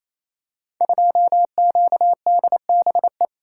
Morse code, heard as a keyed tone, about 27 wpm, 700 Hz. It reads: 2QDBE
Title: Code Groups - 5 Letter:Numbers - 28wpm